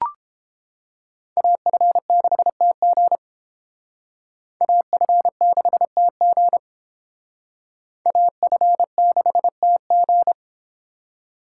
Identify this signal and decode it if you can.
Morse code, heard as a keyed tone, about 31 wpm, 700 Hz. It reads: AF6TG AF6TG AF6TG